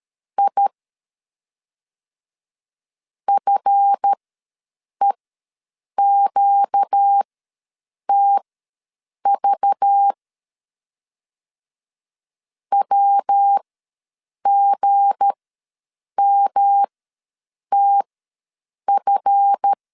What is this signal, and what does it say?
Morse code, heard as a keyed tone, about 13 wpm, spaced as for 7 wpm, 800 Hz. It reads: I FEQTV WGMTF
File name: LEZIONI / 7 GF9Z / 7 GF ws 12wpm 3140